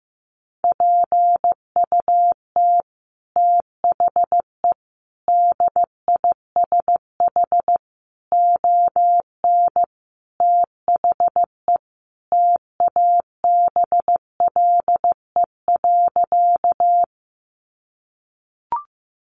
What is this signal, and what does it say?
Morse code, heard as a keyed tone, about 15 wpm, 700 Hz. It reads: PUT THE DISH ON THE TABLE.
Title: Sentences from Top 1300 Words 15wpm